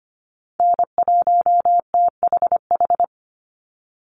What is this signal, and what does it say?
Morse code, heard as a keyed tone, about 25 wpm, 700 Hz. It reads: N1THH
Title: USA Call Signs 25wpm